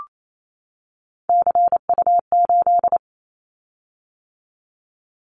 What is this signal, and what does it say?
Morse code, heard as a keyed tone, about 28 wpm, 700 Hz. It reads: CU8
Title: Callsign Prefixes for Top 100 DXCC - 28wpm